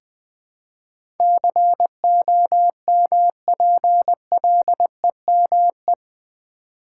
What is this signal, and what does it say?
Morse code, heard as a keyed tone, about 20 wpm, 700 Hz. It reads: COMPLEME
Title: Build A Word - 4 Letters At A Time - 10-Letter Words - 20wpm